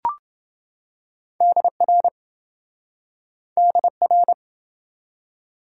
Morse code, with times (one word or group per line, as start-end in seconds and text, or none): 1.4-2.1 DR
3.6-4.3 DR